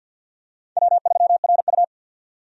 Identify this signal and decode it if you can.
Morse code, heard as a keyed tone, about 48 wpm, 700 Hz. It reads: W3RU
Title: USA Call Signs - 4 Characters - 50wpm